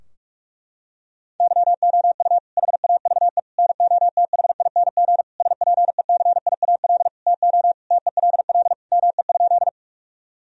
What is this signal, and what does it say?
Morse code, heard as a keyed tone, about 46 wpm, 700 Hz. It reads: YOU HAVE NOTHING SPECIAL TO TELL ME?
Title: Sentences from Top 1000 Words - Review Part 3 - 45wpm